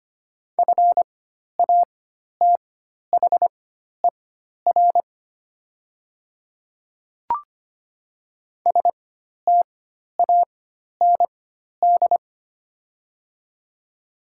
Morse code, heard as a keyed tone, about 25 wpm, spaced as for 12 wpm, 690 Hz. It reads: FATHER STAND
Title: 4x Character Spacing - Top 300 Words - 25wpm